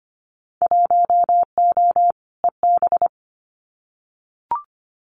Morse code, heard as a keyed tone, about 25 wpm, 700 Hz. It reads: 1O EB
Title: Winter Field Day Exchanges 25wpm